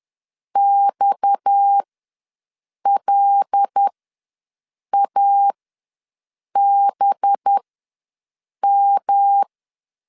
Morse code, heard as a keyed tone, about 10 wpm, spaced as for 6 wpm, 800 Hz. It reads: XLABM
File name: LEZIONI / 6 PKX0 / 6 PKX ws 10wpm 3706